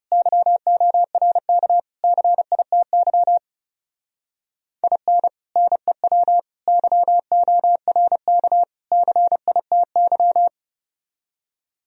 Morse code, heard as a keyed tone, about 39 wpm, 700 Hz. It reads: YORK CITY IN NEW YORK CITY